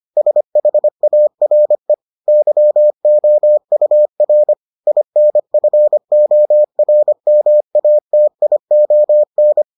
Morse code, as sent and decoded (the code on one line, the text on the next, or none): ... .... .- .-. / -.-- --- ..- .-. / .. -. ..-. --- .-. -- .- - .. --- -.
SHARE YOUR INFORMATION